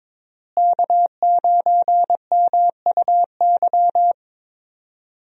Morse code, heard as a keyed tone, about 21 wpm, 700 Hz. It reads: K9MUY
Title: USA Call Signs - 5 Characters - Repeated 3x - 22wpm